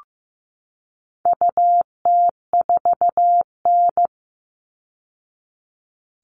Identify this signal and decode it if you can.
Morse code, heard as a keyed tone, about 15 wpm, 700 Hz. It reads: UT4N
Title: All Cut Numbers - 4-Digit 15wpm